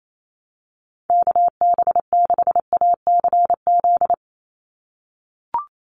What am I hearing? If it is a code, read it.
Morse code, heard as a keyed tone, about 28 wpm, 700 Hz. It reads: KB6ACZ
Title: USA Call Signs 28wpm